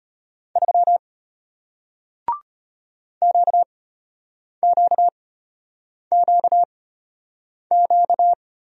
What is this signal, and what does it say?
Morse code, heard as a keyed tone, about 33 wpm, spaced as for 8 wpm, 690 Hz. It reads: 3 QQQQ